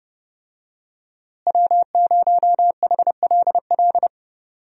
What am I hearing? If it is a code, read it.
Morse code, heard as a keyed tone, about 30 wpm, 700 Hz. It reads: W0HLL
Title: USA Call Signs - 5 Characters - Repeated 2x - 30wpm